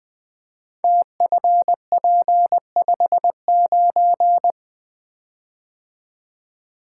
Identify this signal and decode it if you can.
Morse code, heard as a keyed tone, about 20 wpm, 700 Hz. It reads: TFP59
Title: Code Groups - 5 Letter:Numbers - 20wpm